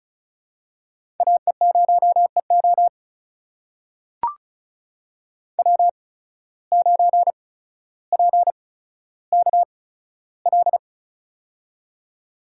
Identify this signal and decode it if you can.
Morse code, heard as a keyed tone, about 34 wpm, spaced as for 15 wpm, 700 Hz. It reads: AE0EO W9PKL
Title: USA Call Signs - 5 Characters - 8x Character Spacing - 35wpm